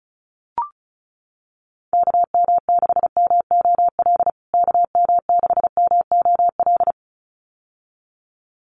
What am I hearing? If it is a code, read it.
Morse code, heard as a keyed tone, about 34 wpm, 700 Hz. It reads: KM6MOL KM6MOL
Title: USA Call Signs Repeated 2x 35wpm